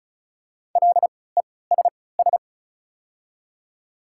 Morse code, heard as a keyed tone, about 34 wpm, spaced as for 20 wpm, 700 Hz. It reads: LESS
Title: 3x Character Spacing - Top 500 Words - 35wpm